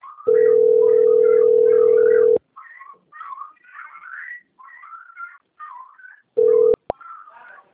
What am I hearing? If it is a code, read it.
Ringback tone (North America), 440 + 480 Hz.
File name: out- - - - - .5023